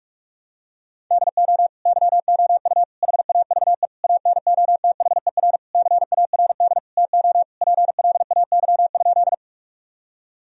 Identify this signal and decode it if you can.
Morse code, heard as a keyed tone, about 42 wpm, 700 Hz. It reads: DO YOU HAVE ANOTHER CARD TO PLAY?
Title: Sentences from Top 1000 Words - Review Part 4 - 45wpm